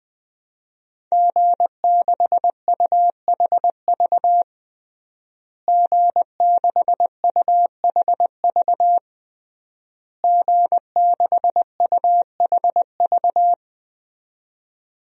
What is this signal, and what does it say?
Morse code, heard as a keyed tone, about 20 wpm, 700 Hz. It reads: G6UHV G6UHV G6UHV
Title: DXCC-100 Call Signs - 5 Characters - Repeated 3x - 20wpm